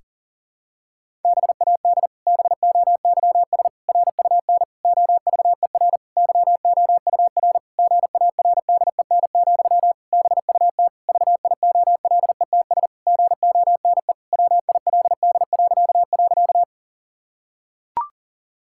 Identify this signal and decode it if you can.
Morse code, heard as a keyed tone, about 40 wpm, 700 Hz. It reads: BAD BOYS RUN OVER YOUR GARDEN, BUT VIOLETS GONE WILD..
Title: Pangram Challenge v2 - 40wpm